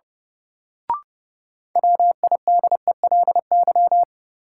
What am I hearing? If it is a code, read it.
Morse code, heard as a keyed tone, about 32 wpm, 700 Hz. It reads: WIDELY
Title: Sentences from Top 1200 Words - New Word Drill Alt - 30wpm